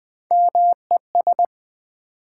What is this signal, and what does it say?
Morse code, heard as a keyed tone, about 20 wpm, 700 Hz. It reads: MES
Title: Build A Word - 1 Letter At A Time - 3 to 5-Letter Words - 20wpm